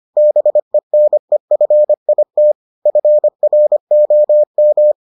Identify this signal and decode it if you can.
Morse code, heard as a keyed tone, about 25 wpm, 600 Hz. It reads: BENEFIT FROM